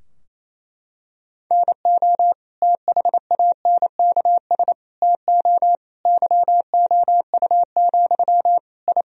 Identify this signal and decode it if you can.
Morse code, heard as a keyed tone, about 28 wpm, 700 Hz. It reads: NO THANKS TO YOU, I
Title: Sentences from Top 1000 Words - Review Part 3 - 28wpm